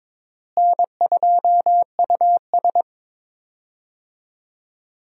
Morse code, heard as a keyed tone, about 22 wpm, 700 Hz. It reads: N2US
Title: USA Call Signs - 4 Characters - Repeated 3x - 22wpm